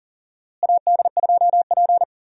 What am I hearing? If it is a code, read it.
Morse code, heard as a keyed tone, about 38 wpm, 700 Hz. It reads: AD2P
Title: USA Call Signs - 4 Characters - Repeated 3x - 40wpm